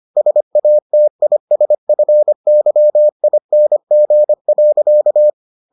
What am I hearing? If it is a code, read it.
Morse code, heard as a keyed tone, about 25 wpm, 600 Hz. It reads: SATISFYING.